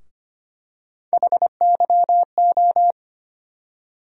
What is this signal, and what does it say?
Morse code, heard as a keyed tone, about 25 wpm, 700 Hz. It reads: HYO
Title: USA Call Sign Postfixes - 3 Letters 25wpm